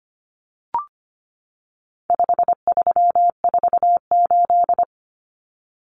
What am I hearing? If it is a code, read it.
Morse code, heard as a keyed tone, about 25 wpm, 700 Hz. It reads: 5348
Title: Common Cut Numbers - 4-Digit 25wpm